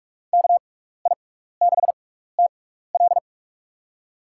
Morse code, heard as a keyed tone, about 46 wpm, spaced as for 16 wpm, 700 Hz. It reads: KI6TL
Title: USA Call Signs - 6x Character Spacing - 45wpm